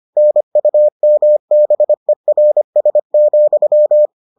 Morse code, heard as a keyed tone, about 25 wpm, 600 Hz. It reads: NUMBERS,